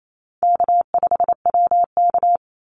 Morse code, heard as a keyed tone, about 28 wpm, 700 Hz. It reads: K5WK
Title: Easy Call Signs - 28wpm